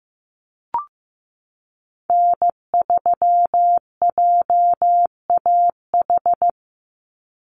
Morse code, heard as a keyed tone, about 15 wpm, 700 Hz. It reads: N3JAH
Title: USA Call Signs 15wpm